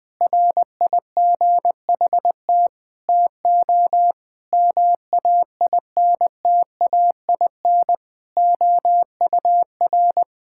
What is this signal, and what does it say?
Morse code, heard as a keyed tone, about 20 wpm, 700 Hz. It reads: RIGHT TO MAINTAIN OUR
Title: Sentences from Top 1000 Words - Review Part 5 - 20wpm